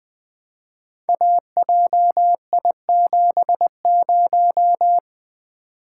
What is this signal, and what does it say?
Morse code, heard as a keyed tone, about 20 wpm, 700 Hz. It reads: AJI70
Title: Code Groups - 5 Letter:Numbers - 20wpm